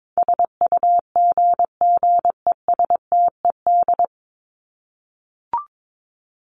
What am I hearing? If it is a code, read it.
Morse code, heard as a keyed tone, about 22 wpm, 700 Hz. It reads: SUGGESTED
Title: ED Suffix - Top 100 Words with Base Words 22wpm